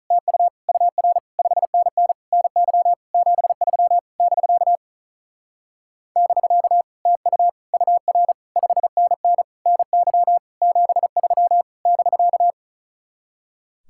Morse code, heard as a keyed tone, about 44 wpm, 700 Hz. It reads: TU UR 5NN NY 73 <BK> <BK> TU UR 5NN NY 73 <BK>